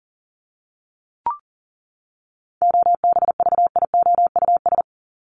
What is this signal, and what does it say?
Morse code, heard as a keyed tone, about 40 wpm, 700 Hz. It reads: OBVIOUS
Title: Top 1501-1750 Words 40wpm